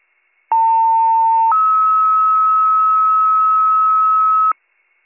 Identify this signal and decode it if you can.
Two-tone page, heard as a one-second tone, then a three-second tone, about 900 then 1290 Hz, with noise. It is heard from 0.5 to 4.5 s.